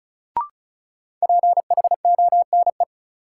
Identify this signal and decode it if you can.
Morse code, heard as a keyed tone, about 36 wpm, 700 Hz. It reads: PHONE
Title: Sentences from Top 1100 Words - New Word Drill - 35wpm